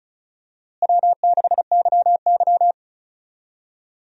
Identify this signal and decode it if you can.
Morse code, heard as a keyed tone, about 34 wpm, 700 Hz. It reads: W6YY
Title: USA Call Signs - 4 Characters - Repeated 2x - 35wpm